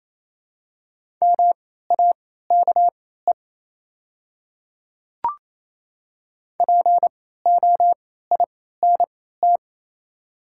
Morse code, heard as a keyed tone, about 28 wpm, spaced as for 16 wpm, 700 Hz. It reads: MAKE POINT
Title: 3x Character Spacing - Top 200 Words - 28wpm